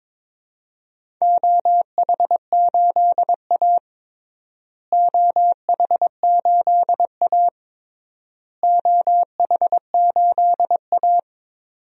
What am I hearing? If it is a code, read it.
Morse code, heard as a keyed tone, about 22 wpm, 700 Hz. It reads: OH8A OH8A OH8A
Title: DXCC-100 Call Signs - 4 Characters - Repeated 3x - 22wpm